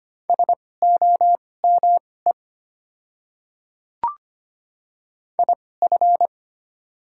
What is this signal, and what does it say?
Morse code, heard as a keyed tone, about 25 wpm, spaced as for 18 wpm, 690 Hz. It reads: SOME IF
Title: 2x Character Spacing - Top 200 Words - 25wpm